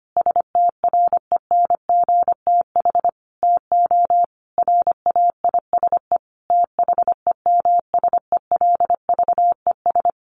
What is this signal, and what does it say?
Morse code, heard as a keyed tone, about 25 wpm, 700 Hz. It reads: STRENGTH TO RAISE THEMSELVES
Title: Sentences from Top 1400 Words 25wpm